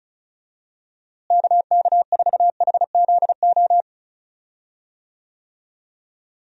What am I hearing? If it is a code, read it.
Morse code, heard as a keyed tone, about 34 wpm, 700 Hz. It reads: KK4HZO
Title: USA Call Signs - 6 Characters - Repeated 2x - 35wpm